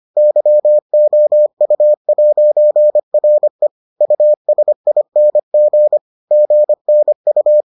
Morse code, heard as a keyed tone, about 25 wpm, 600 Hz. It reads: YOU'RE USING GNU